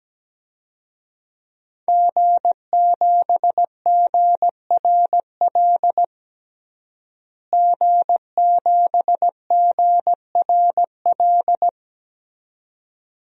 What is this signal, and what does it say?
Morse code, heard as a keyed tone, about 17 wpm, 700 Hz. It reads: G7GRL G7GRL
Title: DXCC-100 Call Signs - 5 Characters - Repeated 2x - 17wpm